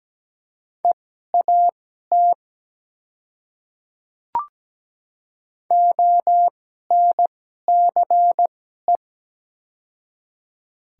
Morse code, heard as a keyed tone, about 17 wpm, spaced as for 12 wpm, 700 Hz. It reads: EAT ONCE
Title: 2x Character Spacing - Top 300 Words - 17wpm